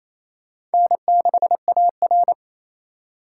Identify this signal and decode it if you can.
Morse code, heard as a keyed tone, about 34 wpm, 700 Hz. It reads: N6AR
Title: Speed Racing - USA 1x2 Call Signs 28wpm